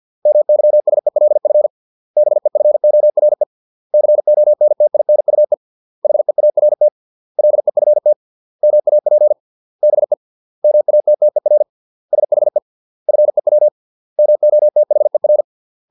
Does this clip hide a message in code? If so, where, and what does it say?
Yes: Morse code from 0.3 to 15.4 s, heard as a keyed tone, about 50 wpm, 600 Hz. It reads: MYSELF BEFORE CONTINUE HEART LEFT MAP BE MATTER SHE FEW MOTHER